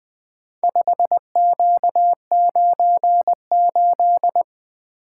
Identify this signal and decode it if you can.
Morse code, heard as a keyed tone, about 20 wpm, 700 Hz. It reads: 5Q98
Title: Four Letter-Number Combinations - 20wpm